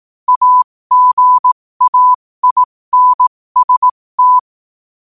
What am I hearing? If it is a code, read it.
Morse code, heard as a keyed tone, about 18 wpm, spaced as for 15 wpm, 1000 Hz. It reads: AGAINST
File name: qotdfiles / 15wpm